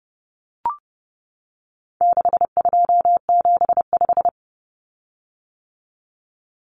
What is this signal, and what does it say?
Morse code, heard as a keyed tone, about 30 wpm, 700 Hz. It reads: 6275
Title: Common Cut Numbers - 4-Digit 30wpm